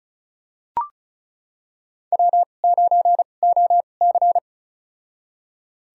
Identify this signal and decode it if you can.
Morse code, heard as a keyed tone, about 36 wpm, spaced as for 25 wpm, 700 Hz. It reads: W9OC
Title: USA Call Signs - 2x Character Spacing - 35wpm